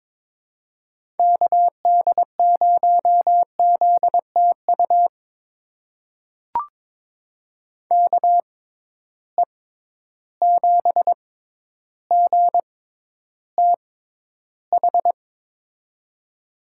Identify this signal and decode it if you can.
Morse code, heard as a keyed tone, about 22 wpm, spaced as for 11 wpm, 700 Hz. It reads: KD0ZTU KE7GTH
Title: USA Call Signs - 6 Characters - 6x Character Spacing - 22wpm